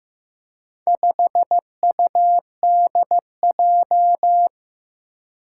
Morse code, heard as a keyed tone, about 15 wpm, 700 Hz. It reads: 5UDJ
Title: Four Challenging Letter-Number Combinations 15wpm